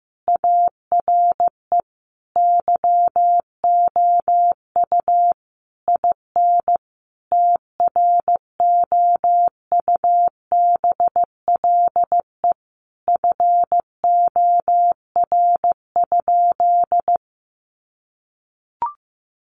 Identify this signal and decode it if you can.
Morse code, heard as a keyed tone, about 15 wpm, 700 Hz. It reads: ARE YOU IN TROUBLE FOR?
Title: Sentences from Top 1200 Words 15wpm